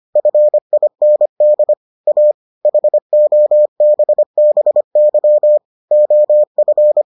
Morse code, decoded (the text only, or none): FIND A HOBBY OF